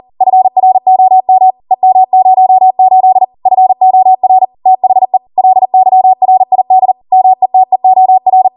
Morse code, heard as a keyed tone, about 40 wpm, 750 Hz. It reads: FROM EM08 FOR THE LYRID METEOR